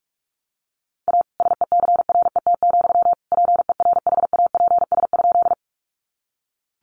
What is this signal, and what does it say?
Morse code, heard as a keyed tone, about 44 wpm, 700 Hz. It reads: A SECRET, PERHAPS?
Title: Sentences from Top 1200 Words 45wpm